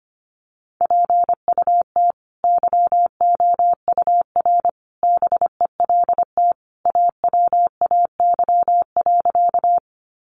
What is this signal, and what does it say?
Morse code, heard as a keyed tone, about 25 wpm, 700 Hz. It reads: PUT YOUR BELT AWAY.
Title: Sentences from Top 1400 Words 25wpm